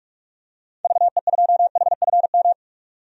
Morse code, heard as a keyed tone, about 42 wpm, 700 Hz. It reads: VE2HFM
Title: DXCC-100 Call Signs - 6 Characters - 45wpm